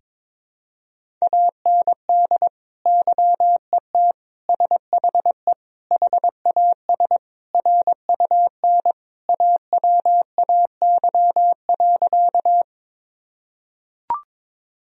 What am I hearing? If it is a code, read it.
Morse code, heard as a keyed tone, about 22 wpm, 700 Hz. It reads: AND YET SHE HAS RUN AWAY.
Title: Sentences from Top 1000 Words - Review Part 1 - 22wpm